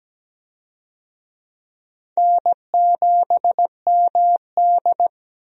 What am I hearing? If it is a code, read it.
Morse code, heard as a keyed tone, about 17 wpm, 700 Hz. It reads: N7MD
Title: USA Call Signs - 4 Characters - 17wpm